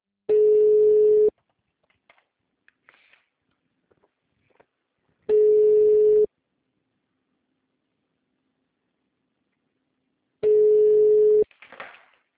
Ringback tone (Europe, 425 Hz).
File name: out- -2024- - - .0